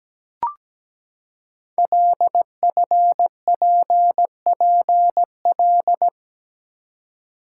Morse code, heard as a keyed tone, about 17 wpm, 700 Hz. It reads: LFPPL
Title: Code Groups - 5 Letters - 17wpm